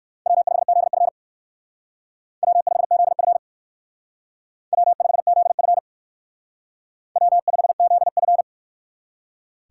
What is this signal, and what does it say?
Morse code, heard as a keyed tone, about 59 wpm, 700 Hz. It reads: W5ZF W5ZF W5ZF W5ZF